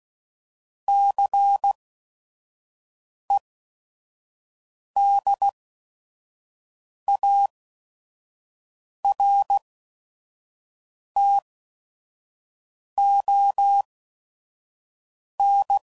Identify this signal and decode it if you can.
Morse code, heard as a keyed tone, about 16 wpm, spaced as for 5 wpm, 800 Hz. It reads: CEDARTON
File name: CW Code Groups @5WPM / fromSM's / Cities